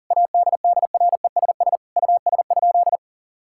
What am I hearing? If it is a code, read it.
Morse code, heard as a keyed tone, about 40 wpm, 700 Hz. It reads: ADDRESS US?
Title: Sentences from Top 1000 Words - Review Part 6 - 40wpm